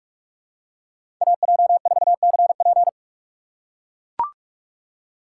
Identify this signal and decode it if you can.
Morse code, heard as a keyed tone, about 42 wpm, 700 Hz. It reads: AJ4CP
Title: USA Call Signs - 5 Characters - 2x Character Spacing - 45wpm